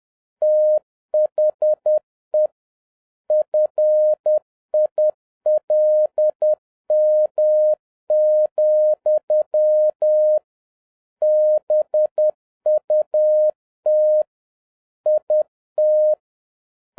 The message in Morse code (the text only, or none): THE FILM, BUT IT